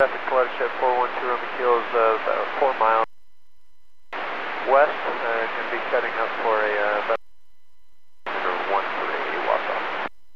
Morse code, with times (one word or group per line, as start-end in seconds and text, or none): none